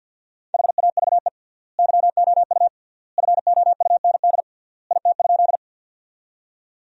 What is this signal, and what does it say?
Morse code, heard as a keyed tone, about 53 wpm, 700 Hz. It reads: HAVE YOU FOUND IT?
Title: 3x Word-Spacing - Sentences from Top 300 Words - 50wpm